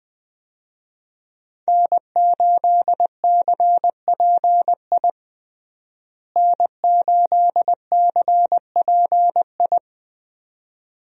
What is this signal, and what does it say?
Morse code, heard as a keyed tone, about 20 wpm, 700 Hz. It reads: N8CPI N8CPI
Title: USA Call Signs - 5 Characters - Repeated 2x - 20wpm